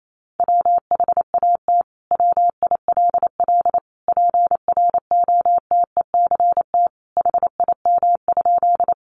Morse code, heard as a keyed tone, about 28 wpm, 700 Hz. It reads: WHAT WILL PROTECT HIM?
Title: Sentences from Top 900 Words 28wpm